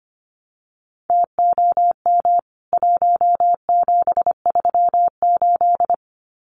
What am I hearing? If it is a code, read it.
Morse code, heard as a keyed tone, about 25 wpm, 700 Hz. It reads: TOM 1738
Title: CWT Exchange 25wpm